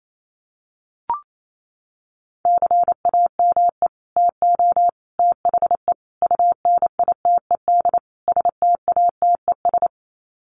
Morse code, heard as a keyed tone, about 28 wpm, 700 Hz. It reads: CAME TO THE UNITED STATES
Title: Sets of 5 Words Encore2 28wpm